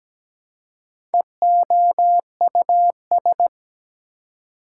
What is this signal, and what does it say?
Morse code, heard as a keyed tone, about 17 wpm, 700 Hz. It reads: EOUS
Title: Prefixes and Suffixes 17wpm